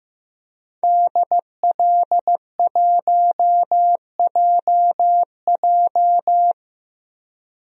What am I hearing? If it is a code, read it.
Morse code, heard as a keyed tone, about 15 wpm, 700 Hz. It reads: DL1JJ
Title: DXCC-100 Call Signs - 5 Characters - 15wpm